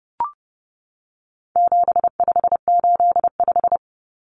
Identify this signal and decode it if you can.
Morse code, heard as a keyed tone, about 30 wpm, 700 Hz. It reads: 7585
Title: Common Cut Numbers - 4-Digit 30wpm